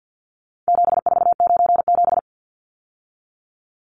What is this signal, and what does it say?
Morse code, heard as a keyed tone, about 50 wpm, 700 Hz. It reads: 7497
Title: Numbers - 4-Digit 50wpm